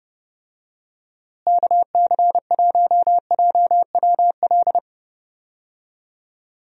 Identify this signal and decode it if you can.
Morse code, heard as a keyed tone, about 30 wpm, 700 Hz. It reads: KC1JWL
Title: USA Call Signs - 6 Characters - Repeated 2x - 30wpm